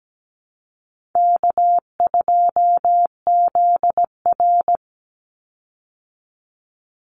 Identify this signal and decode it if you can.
Morse code, heard as a keyed tone, about 17 wpm, 700 Hz. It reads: K2ZR
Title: Call Signs Repeated 2x - 17wpm